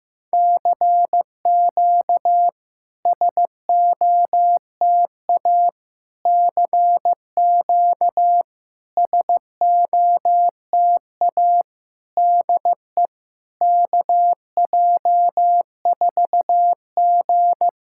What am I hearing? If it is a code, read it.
Morse code, heard as a keyed tone, about 15 wpm, 700 Hz. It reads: CQ SOTA CQ SOTA DE KJ4G
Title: SOTA QSO Practice 15wpm